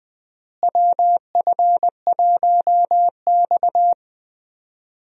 Morse code, heard as a keyed tone, about 20 wpm, 700 Hz. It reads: WF1X
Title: USA Call Signs - 4 Characters - Repeated 2x - 20wpm